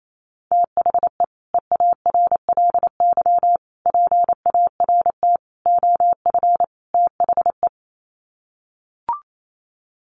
Morse code, heard as a keyed tone, about 28 wpm, 700 Hz. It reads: THE EARLY PART OF THE
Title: Sets of 5 Words Encore2 28wpm